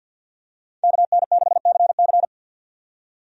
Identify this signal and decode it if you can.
Morse code, heard as a keyed tone, about 47 wpm, 700 Hz. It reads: KN6CC
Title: USA Call Signs - 5 Characters - 50wpm